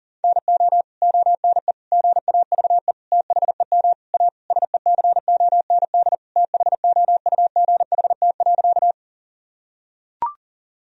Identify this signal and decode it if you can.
Morse code, heard as a keyed tone, about 40 wpm, 700 Hz. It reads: NO ONE GAVE THEM A SECOND THOUGHT.
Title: Sentences from Top 1000 Words - Review Part 1 - 40wpm